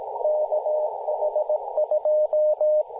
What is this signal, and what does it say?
Morse code, heard as a keyed tone, about 17 wpm, 610 Hz, with radio noise. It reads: KH2
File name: kh2-kh0k-18-082mhz-cw